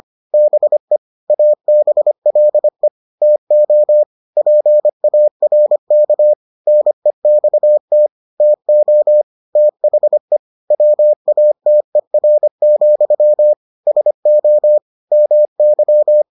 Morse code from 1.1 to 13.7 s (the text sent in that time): ABLE TO PARK NEXT TO THE WATER,